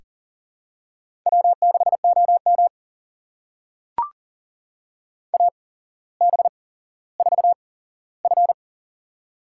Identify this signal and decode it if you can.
Morse code, heard as a keyed tone, about 42 wpm, spaced as for 17 wpm, 700 Hz. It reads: W6OM AB4F
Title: USA Call Signs - 4 Characters - 8x Character Spacing - 40wpm